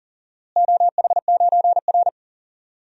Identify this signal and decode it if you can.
Morse code, heard as a keyed tone, about 40 wpm, 700 Hz. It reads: OH9R
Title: DXCC-100 Call Signs - 4 Characters - Repeated 3x - 40wpm